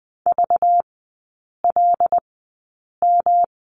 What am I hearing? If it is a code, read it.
Morse code, heard as a keyed tone, about 20 wpm, spaced as for 8 wpm, 700 Hz. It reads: VLM
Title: 191 Lesson 31e - Letter v - Three Characters 20wpm